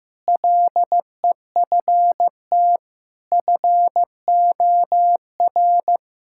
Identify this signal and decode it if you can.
Morse code, heard as a keyed tone, about 15 wpm, 700 Hz. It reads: LEFT FOR
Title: Sentences from Top 1000 Words - Review Part 6 - 15wpm